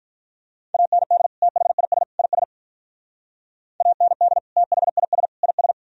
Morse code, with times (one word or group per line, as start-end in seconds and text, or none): 0.7-1.3 AND
1.4-2.0 THIS
2.2-2.4 IS
3.8-4.4 AND
4.6-5.3 THIS
5.4-5.7 IS